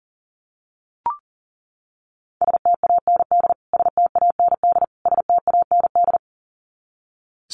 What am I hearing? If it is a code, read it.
Morse code, heard as a keyed tone, about 40 wpm, 700 Hz. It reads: STAND STAND STAND
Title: Top 300 Words Repeated 40wpm